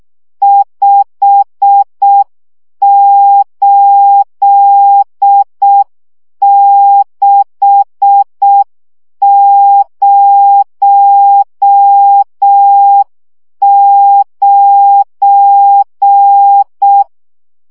Morse code, heard as a keyed tone, about 6 wpm, 800 Hz. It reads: 58609
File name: morze vizsga 2011-3 5